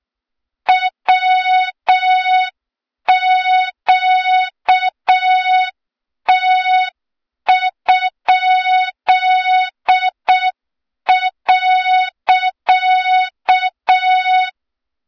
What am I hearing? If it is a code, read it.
Morse code, heard as a keyed tone, about 6 wpm, 3700 Hz, with noise. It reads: WQT?.
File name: Morze-HA2MN / hang / 23 GYAKORLAT